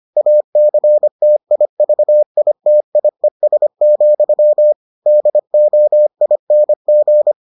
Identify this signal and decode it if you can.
Morse code, heard as a keyed tone, about 25 wpm, 600 Hz. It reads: ACTIVITIES, DOING